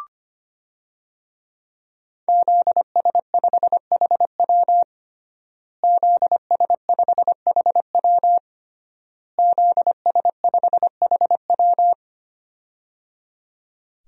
Morse code, heard as a keyed tone, about 25 wpm, 700 Hz. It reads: ZS5HW ZS5HW ZS5HW
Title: DXCC-100 Call Signs - 5 Characters - Repeated 3x - 25wpm